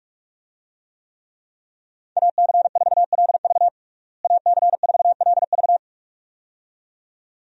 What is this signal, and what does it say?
Morse code, heard as a keyed tone, about 41 wpm, 700 Hz. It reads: AC4LV AC4LV
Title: USA Call Signs - 5 Characters - Repeated 2x - 45wpm